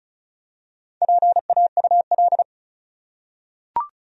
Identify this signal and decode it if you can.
Morse code, heard as a keyed tone, about 36 wpm, 700 Hz. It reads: PAUL
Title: USA Common Names - Max 4 Letters - 35wpm